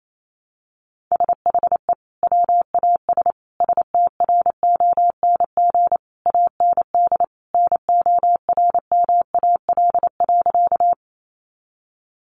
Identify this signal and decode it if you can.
Morse code, heard as a keyed tone, about 28 wpm, 700 Hz. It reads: SHE WAS STRONG AND NORMAL.
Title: Sentences from Top 1200 Words 28wpm